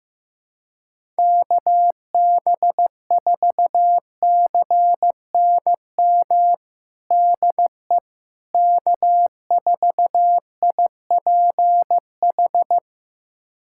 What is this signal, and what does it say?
Morse code, heard as a keyed tone, about 16 wpm, 700 Hz. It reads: KB4CNM DE K4IPH
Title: USA Callsign DE Callsign 15wpm